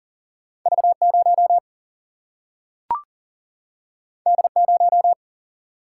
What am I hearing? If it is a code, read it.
Morse code, heard as a keyed tone, about 40 wpm, 700 Hz. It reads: V0 D0
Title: Single-Letter Single-Number 40wpm